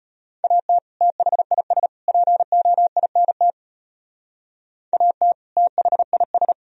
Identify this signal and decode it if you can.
Morse code, heard as a keyed tone, about 34 wpm, 700 Hz. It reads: AT THIS POINT AT THIS